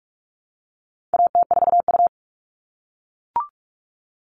Morse code, heard as a keyed tone, about 44 wpm, 700 Hz. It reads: AT4U
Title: All Cut Numbers - 4-Digit 45wpm